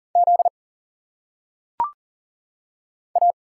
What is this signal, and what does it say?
Morse code, heard as a keyed tone, about 40 wpm, 700 Hz. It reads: Z A
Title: Single Letters - Phonetic Pronunciation 40wpm